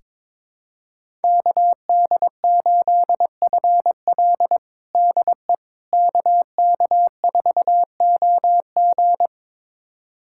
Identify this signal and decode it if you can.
Morse code, heard as a keyed tone, about 22 wpm, 700 Hz. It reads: KD8FL DE KK4OG
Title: USA Callsign DE Callsign - 5 Characters - 22wpm